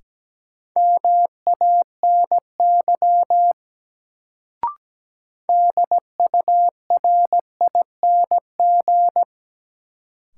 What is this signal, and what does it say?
Morse code, heard as a keyed tone, about 17 wpm, 700 Hz. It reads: MANY DURING
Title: Sets of 2 Words - New Word Drill - 17wpm